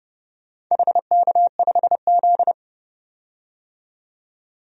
Morse code, heard as a keyed tone, about 30 wpm, 700 Hz. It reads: HK5Z
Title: DXCC-100 Call Signs - 4 Characters - 30wpm